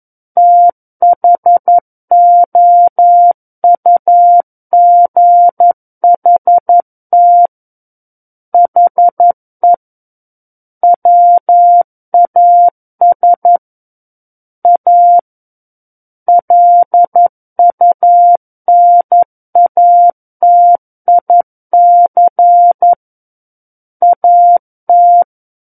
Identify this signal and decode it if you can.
Morse code, heard as a keyed tone, about 11 wpm, 700 Hz. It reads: THOUGHT HE WAS A LUNATIC AT
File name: Book One / 02-Chapter 2 - 11 WPM